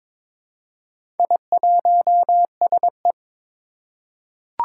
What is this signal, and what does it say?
Morse code, heard as a keyed tone, about 22 wpm, 700 Hz. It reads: I1SE
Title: Four Challenging Letter-Number Combinations 22wpm